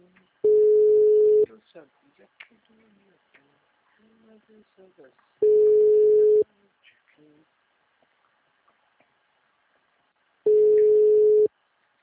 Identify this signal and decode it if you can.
Ringback tone (Europe, 425 Hz), with speech.